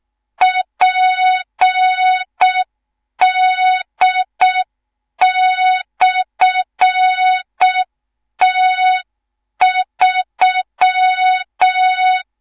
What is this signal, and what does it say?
Morse code, heard as a keyed tone, about 6 wpm, 3700 Hz. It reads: PD/T3
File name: Morze-HA2MN / hang / 18 IRASJELEK2